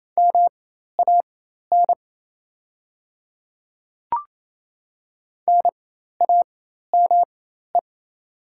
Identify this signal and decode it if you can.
Morse code, heard as a keyed tone, about 28 wpm, spaced as for 13 wpm, 700 Hz. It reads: MAN NAME
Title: 4x Character Spacing - Top 300 Words - 28wpm